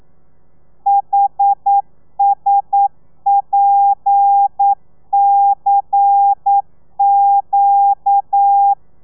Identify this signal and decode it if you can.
Morse code, heard as a keyed tone, about 9 wpm, 800 Hz. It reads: HSPCQ
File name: telegrafia / i2tfj / Lez11 40